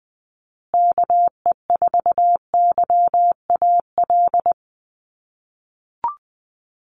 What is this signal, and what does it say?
Morse code, heard as a keyed tone, about 20 wpm, 700 Hz. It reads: KE4YAL
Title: USA Call Signs Repeated 3x 20wpm